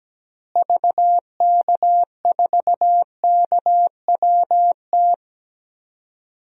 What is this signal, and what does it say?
Morse code, heard as a keyed tone, about 17 wpm, 700 Hz. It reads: VK4KWT
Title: DXCC-100 Call Signs - 6 Characters - Repeated 2x - 17wpm